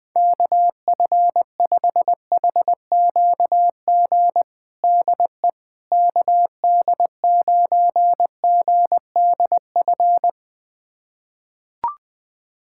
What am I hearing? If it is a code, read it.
Morse code, heard as a keyed tone, about 20 wpm, 700 Hz. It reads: KF5HQG DE KD9GDF
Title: USA Callsign DE Callsign 20wpm